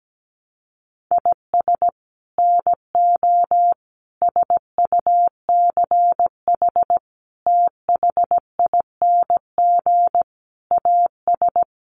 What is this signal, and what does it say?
Morse code, heard as a keyed tone, about 17 wpm, 700 Hz. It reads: IS NO SUCH THING AS
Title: Sets of 5 Words Encore 17wpm